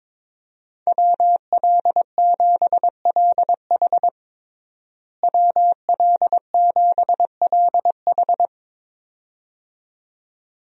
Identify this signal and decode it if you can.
Morse code, heard as a keyed tone, about 22 wpm, 700 Hz. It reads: WL7LH WL7LH
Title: USA Call Signs - 5 Characters - Repeated 2x - 22wpm